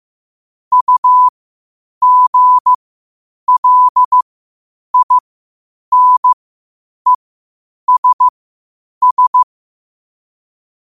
Morse code, heard as a keyed tone, about 15 wpm, spaced as for 8 wpm, 1000 Hz. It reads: UGLINESS
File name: qotdfiles / 8wpm